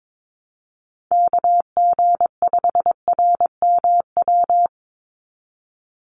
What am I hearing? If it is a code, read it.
Morse code, heard as a keyed tone, about 22 wpm, 700 Hz. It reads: KG5RMW
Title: USA Call Signs 22wpm